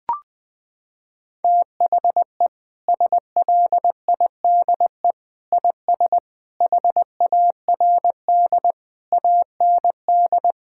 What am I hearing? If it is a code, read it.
Morse code, heard as a keyed tone, about 20 wpm, 700 Hz. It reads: THE SLIDE IS HARD AND
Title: Sentences from Top 1600 Words 20wpm